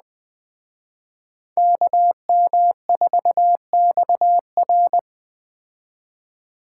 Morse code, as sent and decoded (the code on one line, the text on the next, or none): -.- -- ....- -..- .-.
KM4XR